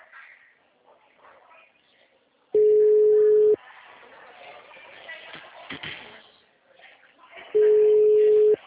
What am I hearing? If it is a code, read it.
Ringback tone (Europe, 425 Hz), with speech.